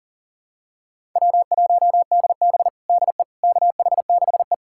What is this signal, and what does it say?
Morse code, heard as a keyed tone, about 37 wpm, 700 Hz. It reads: W1DB DE KH6E
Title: USA Callsign DE Callsign - 4 Characters - 40wpm